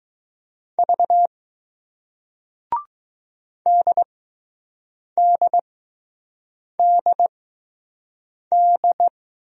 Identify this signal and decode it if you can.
Morse code, heard as a keyed tone, about 20 wpm, spaced as for 6 wpm, 700 Hz. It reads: V DDDD